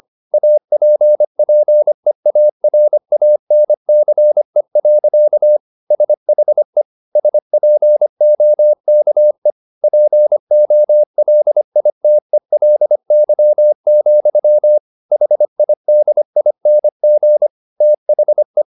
Morse code, heard as a keyed tone, about 25 wpm, 600 Hz. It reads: APPEARANCE. SHE SPOKE POLITELY, HIDING THE